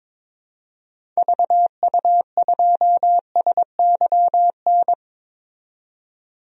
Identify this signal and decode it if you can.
Morse code, heard as a keyed tone, about 22 wpm, 700 Hz. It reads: VU2SYN
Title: DXCC-100 Call Signs - 6 Characters - 22wpm